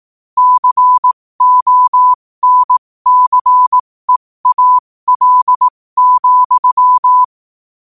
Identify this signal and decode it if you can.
Morse code, heard as a keyed tone, about 18 wpm, spaced as for 15 wpm, 1000 Hz. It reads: CONCEAL,
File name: qotdfiles / 15wpm